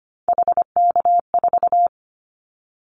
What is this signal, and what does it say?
Morse code, heard as a keyed tone, about 25 wpm, 700 Hz. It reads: HK4
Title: Callsign Prefixes for Top 100 DXCC - 25wpm